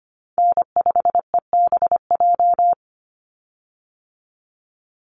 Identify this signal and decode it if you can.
Morse code, heard as a keyed tone, about 25 wpm, 700 Hz. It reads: N5EBJ
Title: USA Call Signs 25wpm